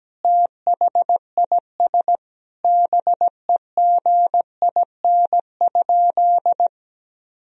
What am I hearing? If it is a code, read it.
Morse code, heard as a keyed tone, about 17 wpm, 700 Hz. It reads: THIS BEGIN?
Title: Sentences from Top 1000 Words - Review Part 1 - 17wpm